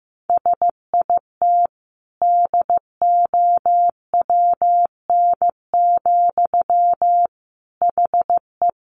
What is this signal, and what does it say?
Morse code, heard as a keyed tone, about 14 wpm, 700 Hz. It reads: SIT DOWN, HE
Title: Sentences from Top 1500 Words 15wpm